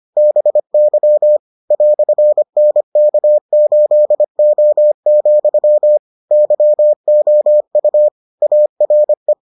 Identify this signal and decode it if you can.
Morse code, heard as a keyed tone, about 25 wpm, 600 Hz. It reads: BY "NK8O, YOU ARE